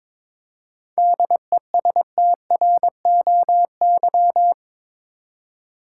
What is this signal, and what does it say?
Morse code, heard as a keyed tone, about 22 wpm, 700 Hz. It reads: DESTROY
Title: Seven Letter Words - 22wpm